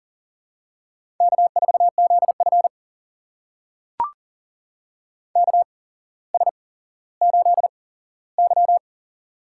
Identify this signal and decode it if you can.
Morse code, heard as a keyed tone, about 40 wpm, spaced as for 17 wpm, 700 Hz. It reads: K4ZF KS8Y
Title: USA Call Signs - 4 Characters - 8x Character Spacing - 40wpm